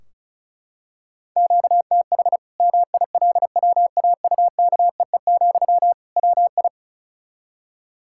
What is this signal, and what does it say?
Morse code, heard as a keyed tone, about 36 wpm, 700 Hz. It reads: QTH MILWAUKEE, WI
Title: Practice QSO - Exchange 1 - Beginning - 35wpm